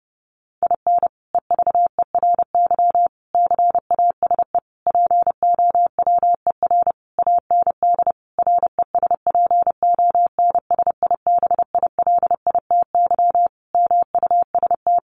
Morse code, heard as a keyed tone, about 28 wpm, 700 Hz. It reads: IN EVERY CASE POWER AND RESPONSIBILITY MUST
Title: Sentences from Top 1200 Words 30wpm